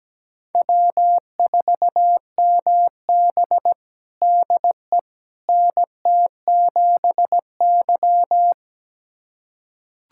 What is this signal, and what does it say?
Morse code, heard as a keyed tone, about 17 wpm, 700 Hz. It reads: W4MB DE NT7Y
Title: USA Callsign DE Callsign - 4 Characters - 17wpm